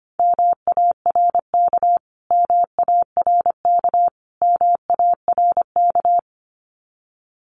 Morse code, heard as a keyed tone, about 25 wpm, 700 Hz. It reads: MARK MARK MARK
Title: Top 500 Words Repeated 25wpm